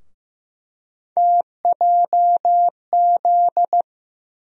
Morse code, heard as a keyed tone, about 15 wpm, 700 Hz. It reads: TJZ
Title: USA Call Sign Postfixes - 3 Letters 15wpm